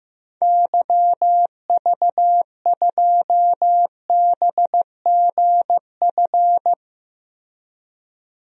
Morse code, heard as a keyed tone, about 15 wpm, 700 Hz. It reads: YV2BGF
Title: DXCC-100 Call Signs - 6 Characters - Repeated 2x - 15wpm